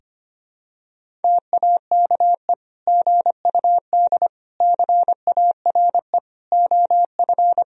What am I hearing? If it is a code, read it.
Morse code, heard as a keyed tone, about 25 wpm, 700 Hz. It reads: TAKE GUD CARE OF
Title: Sentences from Top 400 Words and Abbreviations 25wpm